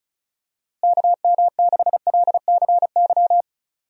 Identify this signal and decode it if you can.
Morse code, heard as a keyed tone, about 34 wpm, 700 Hz. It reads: KM6LCY
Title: USA Call Signs - 6 Characters - Repeated 2x - 35wpm